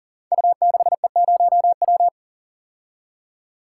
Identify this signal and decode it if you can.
Morse code, heard as a keyed tone, about 38 wpm, 700 Hz. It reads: U6E0W
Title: Code Groups - 5 Letter:Numbers - 40wpm